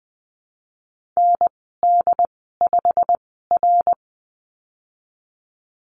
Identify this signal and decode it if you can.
Morse code, heard as a keyed tone, about 20 wpm, spaced as for 15 wpm, 700 Hz. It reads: ND5R
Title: 088 Lesson 15g - Number 5 - Call Signs 20wpm